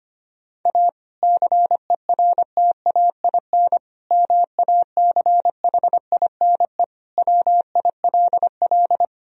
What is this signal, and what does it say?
Morse code, heard as a keyed tone, about 25 wpm, 700 Hz. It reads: A CERTAIN MACHINE WILL
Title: Sentences from Top 1000 Words - Review Part 2 - 25wpm